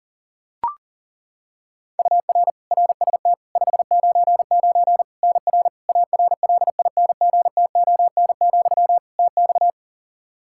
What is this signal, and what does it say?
Morse code, heard as a keyed tone, about 40 wpm, 700 Hz. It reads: UR RST 599 NR ARLINGTON, TX
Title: Practice QSO - Exchange 1 - Beginning - 40wpm